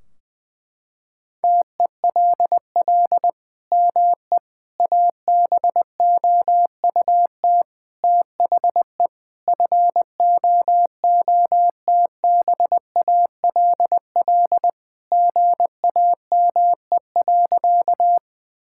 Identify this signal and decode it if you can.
Morse code, heard as a keyed tone, about 20 wpm, 700 Hz. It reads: TELL ME ABOUT THE FOOTBALL GAME.
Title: Sentences from Top 1600 Words 20wpm